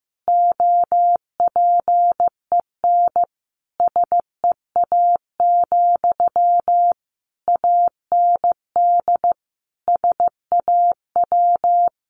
Morse code, heard as a keyed tone, about 15 wpm, 700 Hz. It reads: OPEN SEA, AND SAW
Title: Sentences from Top 1200 Words 15wpm